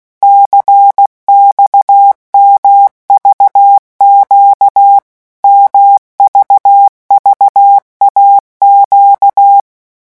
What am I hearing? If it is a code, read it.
Morse code, heard as a keyed tone, about 16 wpm, 800 Hz. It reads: CXMVQ MVVAQ
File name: morse mp3 / revision2-vitesse 15 mots